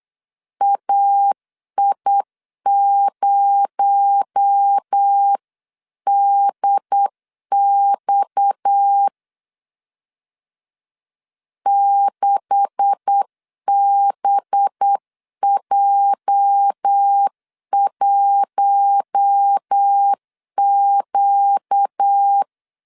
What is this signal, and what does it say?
Morse code, heard as a keyed tone, about 8 wpm, 800 Hz. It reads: AI0DX 6BJ1Q
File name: CW / DET / 1 9 1500C S A 8wpm 5637